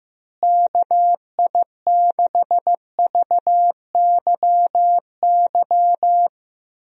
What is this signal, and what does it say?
Morse code, heard as a keyed tone, about 15 wpm, 700 Hz. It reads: KI6VYY
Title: USA Call Signs - 6 Characters - Repeated 3x - 15wpm